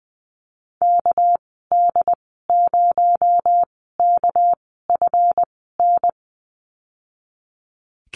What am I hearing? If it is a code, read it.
Morse code, heard as a keyed tone, about 20 wpm, spaced as for 14 wpm, 700 Hz. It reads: KD0KFN
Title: 231 Lesson 37g - Letter x - Call Signs 20wpm